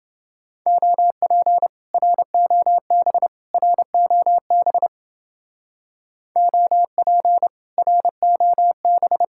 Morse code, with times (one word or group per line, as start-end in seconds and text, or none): 0.7-1.7 OP
1.9-3.3 ROB
3.5-4.9 ROB
6.4-7.5 OP
7.8-9.2 ROB